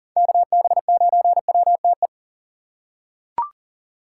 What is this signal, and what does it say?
Morse code, heard as a keyed tone, about 40 wpm, 700 Hz. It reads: KB9WTE
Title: USA Call Signs - 6 Characters - 4x Character Spacing - 40wpm